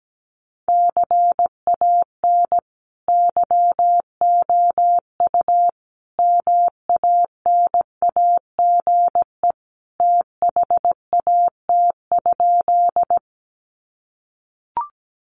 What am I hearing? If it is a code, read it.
Morse code, heard as a keyed tone, about 17 wpm, 700 Hz. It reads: CAN YOU MANAGE THAT?
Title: Sentences from Top 1400 Words 17wpm